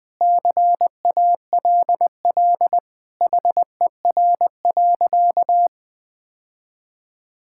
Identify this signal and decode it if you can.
Morse code, heard as a keyed tone, about 20 wpm, 700 Hz. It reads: CALL HER.
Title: Sentences with 4-Word Phrases - 20wpm